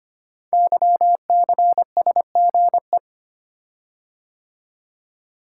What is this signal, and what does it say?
Morse code, heard as a keyed tone, about 25 wpm, 700 Hz. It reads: YCSGE
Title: Code Groups - 5 Letters - 25wpm